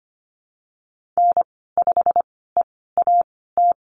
Morse code, heard as a keyed tone, about 25 wpm, spaced as for 16 wpm, 700 Hz. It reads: N5EAT
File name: Lesson 15g - Number 5 - Call Signs 25wpm